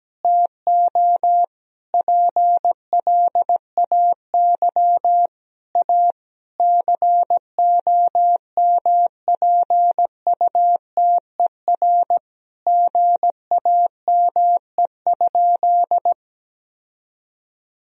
Morse code, heard as a keyed tone, about 17 wpm, 700 Hz. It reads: TO PLAY A COMPUTER GAME?
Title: Sentences from Top 1000 Words - Review Part 3 - 17wpm